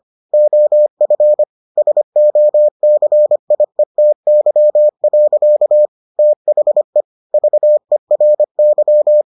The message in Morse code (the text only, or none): OF SOCIETY. THE VERY